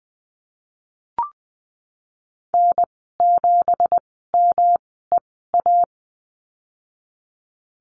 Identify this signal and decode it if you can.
Morse code, heard as a keyed tone, about 20 wpm, spaced as for 15 wpm, 700 Hz. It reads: N7MEA
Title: 167 Lesson 27g - Number 7 - Call Signs 20wpm